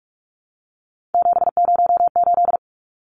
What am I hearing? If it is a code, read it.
Morse code, heard as a keyed tone, about 44 wpm, 700 Hz. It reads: 708
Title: Numbers - 3-Digit 45wpm